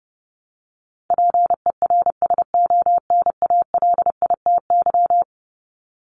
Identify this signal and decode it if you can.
Morse code, heard as a keyed tone, about 30 wpm, 700 Hz. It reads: PERSONALITY
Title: Top 1251-1500 Words 30wpm